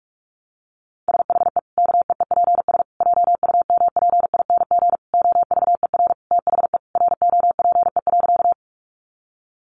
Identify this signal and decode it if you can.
Morse code, heard as a keyed tone, about 44 wpm, 700 Hz. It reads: SHE KEEPS JUMPING OVER THE ROPE.
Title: Easy Phrases 45wpm